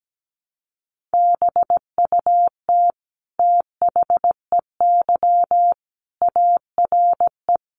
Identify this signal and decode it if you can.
Morse code, heard as a keyed tone, about 17 wpm, 700 Hz. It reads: BUT THEY ARE
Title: Sentences from Top 1300 Words 17wpm